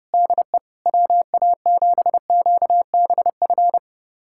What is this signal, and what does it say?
Morse code, heard as a keyed tone, about 30 wpm, 700 Hz. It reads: DE WA7QBF
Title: USA Callsign DE Callsign 30wpm